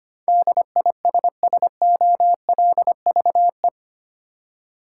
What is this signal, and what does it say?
Morse code, heard as a keyed tone, about 25 wpm, 700 Hz. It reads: DISSOLVE
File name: 8 Letter Words - First 3 Letters Given - 25wpm